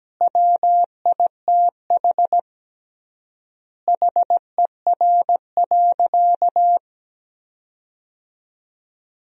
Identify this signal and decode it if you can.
Morse code, heard as a keyed tone, about 17 wpm, 700 Hz. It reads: WITH HER.
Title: 3x Word-Spacing - Sentences from Top 100 Words - 17wpm